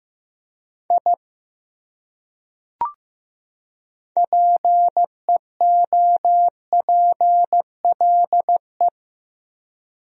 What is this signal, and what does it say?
Morse code, heard as a keyed tone, about 15 wpm, 700 Hz. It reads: I PEOPLE
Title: Top 100 Words 15wpm